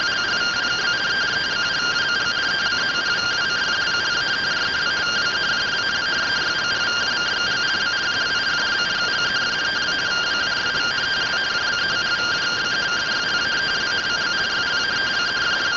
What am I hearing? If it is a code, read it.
Two-tone teleprinter signal, about 170 Hz shift.